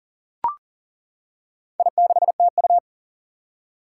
Morse code, heard as a keyed tone, about 40 wpm, 700 Hz. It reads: I6TU
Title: Four Challenging Letter-Number Combinations 40wpm